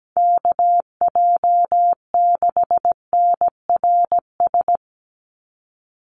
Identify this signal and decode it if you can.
Morse code, heard as a keyed tone, about 17 wpm, 700 Hz. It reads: KJ6NRS
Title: USA Call Signs 17wpm